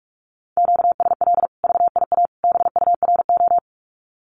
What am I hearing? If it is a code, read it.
Morse code, heard as a keyed tone, about 44 wpm, 700 Hz. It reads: QSL VIA BURO
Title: Common QSO Phrases 45wpm